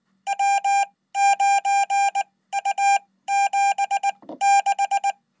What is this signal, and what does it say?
Morse code, heard as a keyed tone, about 20 wpm, spaced as for 15 wpm, 800 Hz. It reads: W9U76
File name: Code